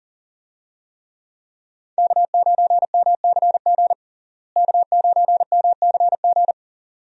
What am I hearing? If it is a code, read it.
Morse code, heard as a keyed tone, about 39 wpm, 700 Hz. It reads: K9MCG K9MCG
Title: USA Call Signs - 5 Characters - Repeated 2x - 40wpm